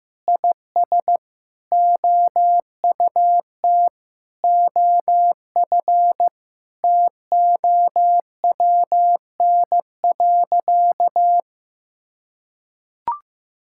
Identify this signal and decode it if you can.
Morse code, heard as a keyed tone, about 15 wpm, 700 Hz. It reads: IS OUT OF TOWN.
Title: Sentences from Top 1000 Words - Review Part 5 - 15wpm